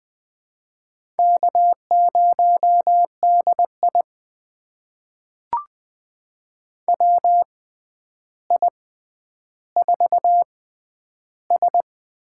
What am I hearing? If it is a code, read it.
Morse code, heard as a keyed tone, about 20 wpm, spaced as for 10 wpm, 700 Hz. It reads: K0DI WI4S
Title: USA Call Signs - 4 Characters - 6x Character Spacing - 20wpm